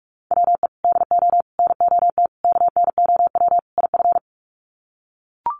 Morse code, heard as a keyed tone, about 44 wpm, 700 Hz. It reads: WE DO NOT KNOW IF
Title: Sets of 5 Words Encore2 45wpm